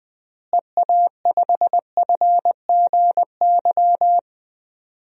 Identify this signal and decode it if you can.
Morse code, heard as a keyed tone, about 20 wpm, 700 Hz. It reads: EA5FGY
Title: DXCC-100 Call Signs - 6 Characters - Repeated 2x - 20wpm